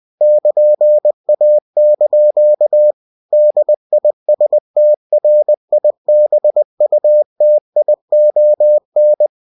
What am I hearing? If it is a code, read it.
Morse code, heard as a keyed tone, about 20 wpm, 600 Hz. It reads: <KN>A) DISTRIBUTION
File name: cw-books / winnie-the-pooh / winnie-20wpm-page-0120